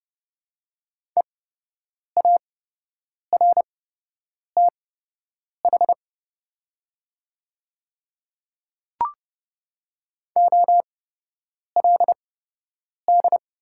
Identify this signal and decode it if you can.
Morse code, heard as a keyed tone, about 32 wpm, spaced as for 8 wpm, 700 Hz. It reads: EARTH OLD
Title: 8x Character Spacing - Top 500 Words - 30wpm